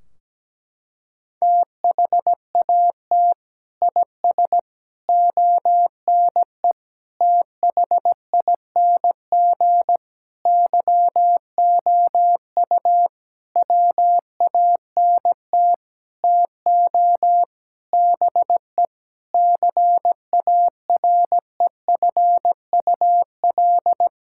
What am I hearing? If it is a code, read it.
Morse code, heard as a keyed tone, about 17 wpm, 700 Hz. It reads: THAT IS ONE THING YOU WANT TO BE CAREFUL
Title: Sentences from Top 1600 Words 17wpm